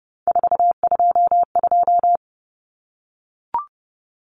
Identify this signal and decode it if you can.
Morse code, heard as a keyed tone, about 30 wpm, 700 Hz. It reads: 422
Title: Numbers - 3-Digit 30wpm